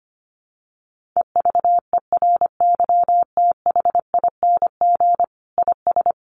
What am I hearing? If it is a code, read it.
Morse code, heard as a keyed tone, about 25 wpm, 700 Hz. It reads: EVERYTHING IS
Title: Sentences from Top 1400 Words 25wpm